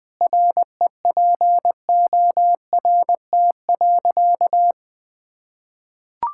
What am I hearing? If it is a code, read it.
Morse code, heard as a keyed tone, about 21 wpm, 700 Hz. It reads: REPORT.
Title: Sentences from Top 1000 Words - Review Part 6 - 20wpm